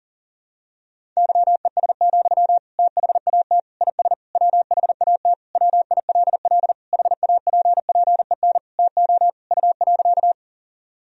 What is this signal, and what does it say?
Morse code, heard as a keyed tone, about 37 wpm, 700 Hz. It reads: YES, THAT IS WHAT WILL HAPPEN TO U.
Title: Sentences from Top 500 Words and Abbreviations 40wpm